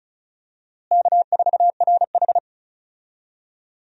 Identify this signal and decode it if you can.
Morse code, heard as a keyed tone, about 36 wpm, 700 Hz. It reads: K4RH
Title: USA Call Signs - 4 Characters - Repeated 2x - 35wpm